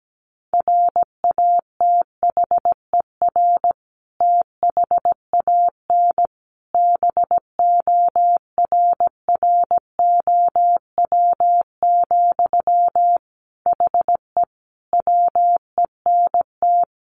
Morse code, heard as a keyed tone, about 17 wpm, 700 Hz. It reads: RATHER THAN BORROW, HE WENT
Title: Sentences from Top 1500 Words 17wpm